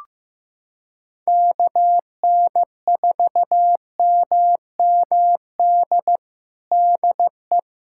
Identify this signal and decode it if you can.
Morse code, heard as a keyed tone, about 16 wpm, 700 Hz. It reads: KN4MMD DE
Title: USA Callsign DE Callsign - 6 Characters - 15wpm